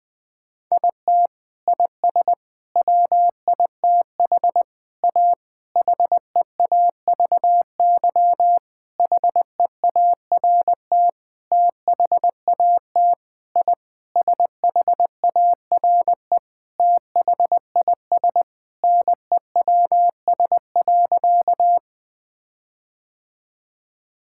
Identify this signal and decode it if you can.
Morse code, heard as a keyed tone, about 20 wpm, 700 Hz. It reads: IT IS WITH A HEAVY HEART THAT I SHARE THIS NEWS.
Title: Sentences with 8-Word Phrases - Spoken Beginning - 20wpm